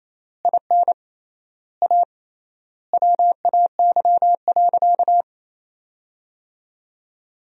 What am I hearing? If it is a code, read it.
Morse code, heard as a keyed tone, about 28 wpm, 700 Hz. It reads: IN A WAY.
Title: 3x Word-Spacing - Sentences from Top 500 Words - 28wpm